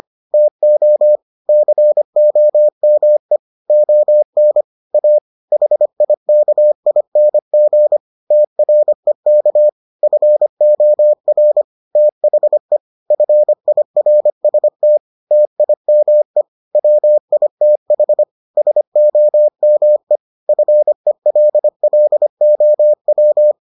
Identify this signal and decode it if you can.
Morse code, heard as a keyed tone, about 25 wpm, 600 Hz. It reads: TO COME ON A HIKING TREK FOR THE FIRST TIME WITH SOME FELLOW